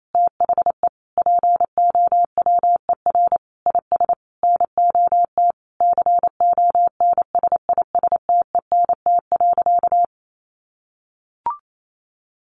Morse code, heard as a keyed tone, about 28 wpm, 700 Hz. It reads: THE POWER IS NOT CONSISTENT.
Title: Sentences from Top 1500 Words 28wpm